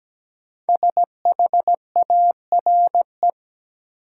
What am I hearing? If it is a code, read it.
Morse code, heard as a keyed tone, about 17 wpm, 700 Hz. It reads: SHARE
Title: Sentences from Top 900 Words - New Word Drill - 17wpm